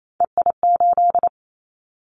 Morse code, heard as a keyed tone, about 28 wpm, 700 Hz. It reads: EI8
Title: Callsign Prefixes for Top 50 DXCC - 28wpm